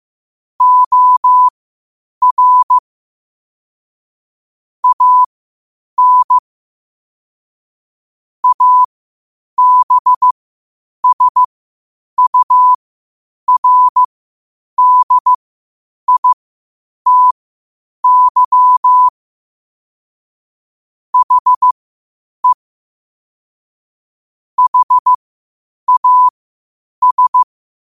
Morse code, heard as a keyed tone, about 15 wpm, spaced as for 8 wpm, 1000 Hz. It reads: OR AN ABSURDITY HE HAS
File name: 8wpm